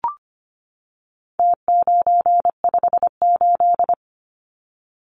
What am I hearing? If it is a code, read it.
Morse code, heard as a keyed tone, about 25 wpm, 700 Hz. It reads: T958
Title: Numbers - Contest Style - 4-Digit 25wpm